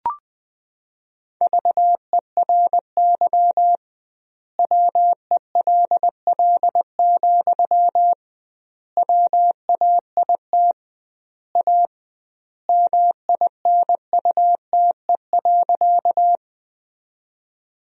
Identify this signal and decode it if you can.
Morse code, heard as a keyed tone, about 20 wpm, 700 Hz. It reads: VERY WELL, WAIT A MINUTE.
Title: 2x Word-Spacing - Sentences from Top 500 Words - 20wpm